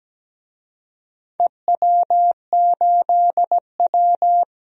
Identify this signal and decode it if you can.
Morse code, heard as a keyed tone, about 17 wpm, 700 Hz. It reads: EW8W
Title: DXCC-100 Call Signs - 4 Characters - 17wpm